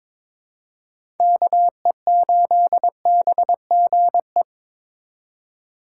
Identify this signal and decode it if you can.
Morse code, heard as a keyed tone, about 22 wpm, 700 Hz. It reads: KE8BGE
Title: USA Call Signs - 6 Characters - 2x Character Spacing - 22wpm